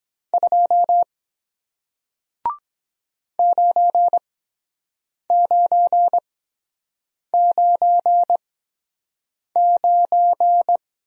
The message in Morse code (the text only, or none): 2 9999